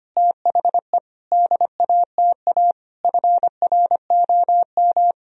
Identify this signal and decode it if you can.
Morse code, heard as a keyed tone, about 25 wpm, 700 Hz. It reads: THE DATA FROM
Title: Sentences from Top 1000 Words - Review Part 3 - 25wpm